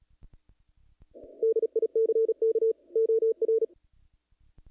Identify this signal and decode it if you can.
Morse code, heard as a keyed tone, about 35 wpm, 440 Hz. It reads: DICK OR
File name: contests / naqp-cw-k5tr-op-k5pi- -1900